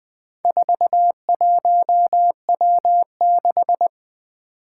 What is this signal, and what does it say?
Morse code, heard as a keyed tone, about 20 wpm, 700 Hz. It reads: 41W6
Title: Four Challenging Letter-Number Combinations 20wpm